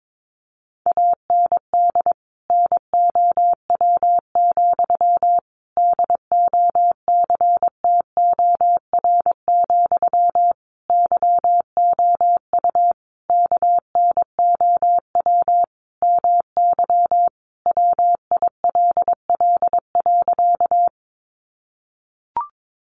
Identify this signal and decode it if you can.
Morse code, heard as a keyed tone, about 22 wpm, 700 Hz. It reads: AND NOW, DOCTOR, YOU KNOW MY WILL.
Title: Sentences from Top 1500 Words 22wpm